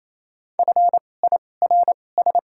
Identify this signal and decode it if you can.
Morse code, heard as a keyed tone, about 27 wpm, spaced as for 20 wpm, 700 Hz. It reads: FIRS
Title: 2x Character Spacing - Top 200 Words - 28wpm